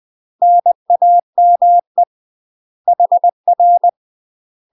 Morse code, heard as a keyed tone, about 20 wpm, 700 Hz. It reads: NAME HR
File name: UK MASTER QSOs TRAINING / 20WPM / UK MASTER QSOs 0046